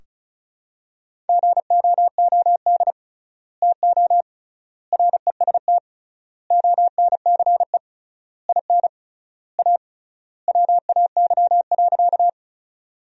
Morse code, heard as a keyed tone, about 36 wpm, 700 Hz. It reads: GOOD TO REST ONCE IN A WAY.
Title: 3x Word-Spacing - Sentences from Top 500 Words - 35wpm